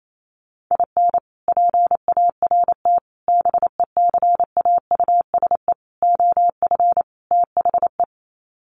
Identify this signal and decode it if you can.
Morse code, heard as a keyed tone, about 28 wpm, 700 Hz. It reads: IN PART BECAUSE OF THE
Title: Sets of 5 Words Encore2 28wpm